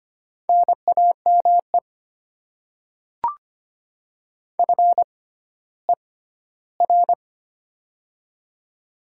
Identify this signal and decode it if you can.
Morse code, heard as a keyed tone, about 25 wpm, spaced as for 14 wpm, 700 Hz. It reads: NAME FER
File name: Top 20 QSO Elements - 6x Character Spacing - 25wpm